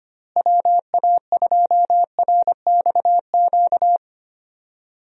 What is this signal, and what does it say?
Morse code, heard as a keyed tone, about 25 wpm, 700 Hz. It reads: WA2RXQ
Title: USA Call Signs - 6 Characters - Repeated 2x - 25wpm